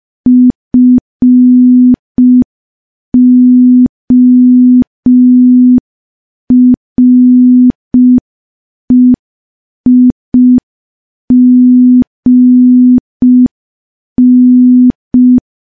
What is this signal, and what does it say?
Morse code, heard as a keyed tone, about 5 wpm, 250 Hz. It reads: FOREIGN
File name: morse / morse-training-05WPM